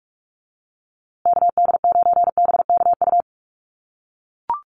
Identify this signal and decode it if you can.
Morse code, heard as a keyed tone, about 44 wpm, 700 Hz. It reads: KD9BKU
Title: USA Call Signs Repeated 2x 45wpm